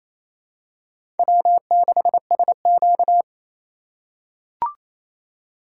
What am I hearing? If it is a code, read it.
Morse code, heard as a keyed tone, about 28 wpm, 700 Hz. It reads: W6SQ
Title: USA Call Signs - 4 Characters - 2x Character Spacing - 28wpm